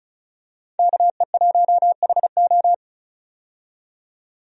Morse code, heard as a keyed tone, about 36 wpm, 700 Hz. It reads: KE1HO
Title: USA Call Signs - 5 Characters - 35wpm